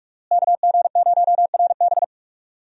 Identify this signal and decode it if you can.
Morse code, heard as a keyed tone, about 46 wpm, 700 Hz. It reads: KG0RB
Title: USA Call Signs - 5 Characters - Repeated 3x - 45wpm